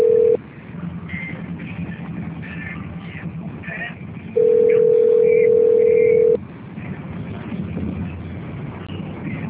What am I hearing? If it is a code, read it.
Ringback tone (North America), 440 + 480 Hz.